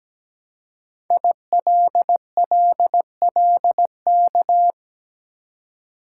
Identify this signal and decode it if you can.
Morse code, heard as a keyed tone, about 17 wpm, 700 Hz. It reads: ILLLK
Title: Code Groups - 5 Letters - 17wpm